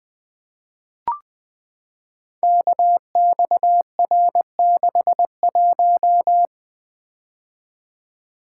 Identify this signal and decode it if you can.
Morse code, heard as a keyed tone, about 20 wpm, 700 Hz. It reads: KXR61
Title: Code Groups - 5 Letter:Numbers - 20wpm